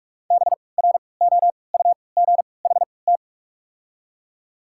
Morse code, heard as a keyed tone, about 43 wpm, spaced as for 25 wpm, 700 Hz. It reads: BROUGHT
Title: 3x Character Spacing - Top 500 Words - 45wpm